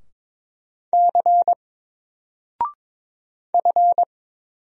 Morse code, heard as a keyed tone, about 22 wpm, 700 Hz. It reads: C F
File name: Single-Letter-Number-22wpm